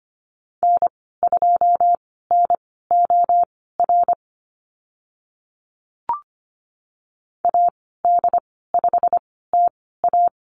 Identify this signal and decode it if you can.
Morse code, heard as a keyed tone, about 25 wpm, spaced as for 16 wpm, 690 Hz. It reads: N2NOR AD5TA
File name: Lesson 15g - Number 5 - Call Signs 25wpm